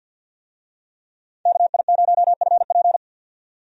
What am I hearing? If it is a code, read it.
Morse code, heard as a keyed tone, about 52 wpm, 700 Hz. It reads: KI0FP
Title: USA Call Signs - 5 Characters - 50wpm